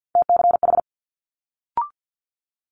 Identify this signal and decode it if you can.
Morse code, heard as a keyed tone, about 50 wpm, 700 Hz. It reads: TCH
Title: Multi-letter Phonemes Quick Fire 50wpm